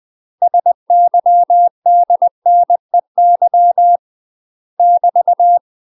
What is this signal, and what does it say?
Morse code, heard as a keyed tone, about 20 wpm, 700 Hz. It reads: SYDNEY <BT>
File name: UK MASTER QSOs TRAINING / 20WPM / UK MASTER QSOs 0059